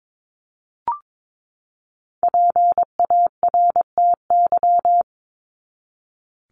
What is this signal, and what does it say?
Morse code, heard as a keyed tone, about 22 wpm, 700 Hz. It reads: PARTY
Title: ES:S Suffix - Top 100 Words with Base Words 22wpm